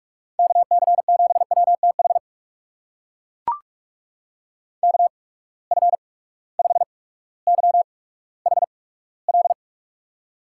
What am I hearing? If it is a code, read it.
Morse code, heard as a keyed tone, about 43 wpm, spaced as for 19 wpm, 700 Hz. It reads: KC7WTH KF5YHL
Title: USA Call Signs - 6 Characters - 8x Character Spacing - 45wpm